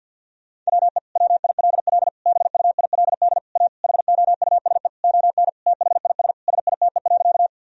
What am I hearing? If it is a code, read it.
Morse code, heard as a keyed tone, about 50 wpm, 700 Hz. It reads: WE WILL BUILD A HOUSE ON THIS SITE.
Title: Sentences from Top 1000 Words - Review Part 4 - 50wpm